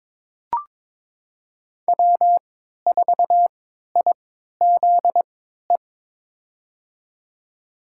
Morse code, heard as a keyed tone, about 22 wpm, spaced as for 13 wpm, 700 Hz. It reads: W4IZE
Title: USA Call Signs - 3x Character Spacing - 22wpm